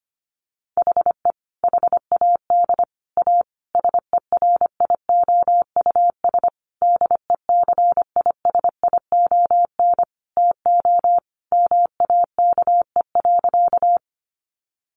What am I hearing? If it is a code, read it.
Morse code, heard as a keyed tone, about 25 wpm, 700 Hz. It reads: HE HAD A SERIOUS DECISION TO MAKE.
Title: Sentences from Top 1100 Words 25wpm